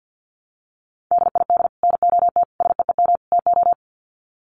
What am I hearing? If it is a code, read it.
Morse code, heard as a keyed tone, about 49 wpm, 700 Hz. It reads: DID NOT SEEM TO
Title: Sets of 4 Words 50wpm